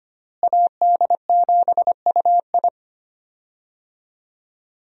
Morse code, heard as a keyed tone, about 25 wpm, 700 Hz. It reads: AD7UI
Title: USA Call Signs - 5 Characters - Repeated 3x - 25wpm